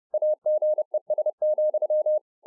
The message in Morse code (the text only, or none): AGES,